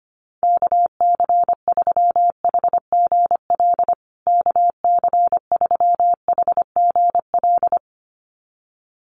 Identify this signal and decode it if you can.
Morse code, heard as a keyed tone, about 25 wpm, 700 Hz. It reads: KC3HGL KC3HGL
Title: USA Call Signs Repeated 2x 25wpm